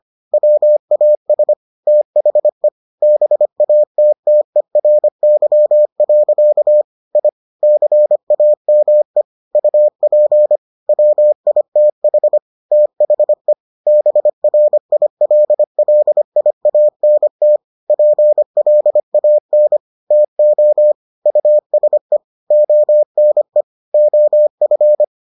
Morse code, heard as a keyed tone, about 25 wpm, 600 Hz. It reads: WAS THE BATTERY. I CAME UP WITH THE BRILLIANT PLAN TO USE ONE OF